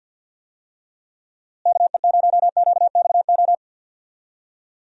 Morse code, heard as a keyed tone, about 50 wpm, 700 Hz. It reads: KE0QXO
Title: USA Call Signs - 6 Characters - Repeated 3x - 50wpm